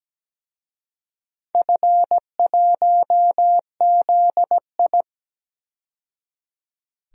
Morse code, heard as a keyed tone, about 17 wpm, 700 Hz. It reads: F1ZI
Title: DXCC-100 Call Signs - 4 Characters - 17wpm